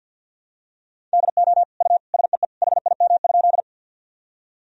Morse code, heard as a keyed tone, about 50 wpm, 700 Hz. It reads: DO U SEE HIM?
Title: Sentences from Top 100 Words and Abbreviations 50wpm